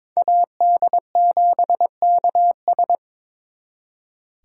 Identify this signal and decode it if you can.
Morse code, heard as a keyed tone, about 26 wpm, 700 Hz. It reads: AD7KS
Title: Speed Racing - USA 2x2 Call Signs 22wpm